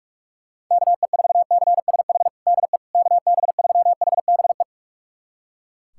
Morse code, heard as a keyed tone, about 41 wpm, 700 Hz. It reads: KE4CSH DE KB3HBE
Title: USA Callsign DE Callsign 45wpm